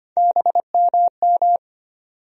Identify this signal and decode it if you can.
Morse code, heard as a keyed tone, about 25 wpm, 700 Hz. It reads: BMM
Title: USA Call Sign Postfixes - 3 Letters 25wpm